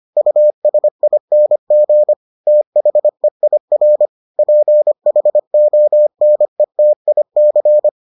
Morse code, heard as a keyed tone, about 25 wpm, 600 Hz. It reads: USING THEIR PHONETIC